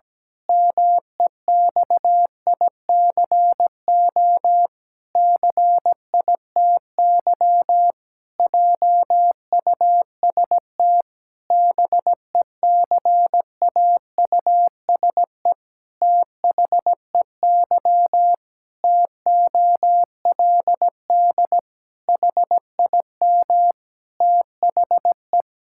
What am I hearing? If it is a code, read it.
Morse code, heard as a keyed tone, about 17 wpm, 700 Hz. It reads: MEXICO CITY JUST BECAUSE THEY TOLD HIM THE
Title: Pangram Challenge v2 - 17wpm